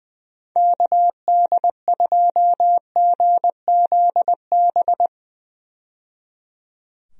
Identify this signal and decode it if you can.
Morse code, heard as a keyed tone, about 20 wpm, 700 Hz. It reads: KD2GZB
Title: USA Call Signs - 6 Characters - 20wpm